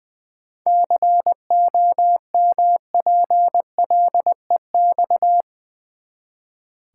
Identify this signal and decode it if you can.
Morse code, heard as a keyed tone, about 19 wpm, 700 Hz. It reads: COMPLEX
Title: Seven Letter Words - 20wpm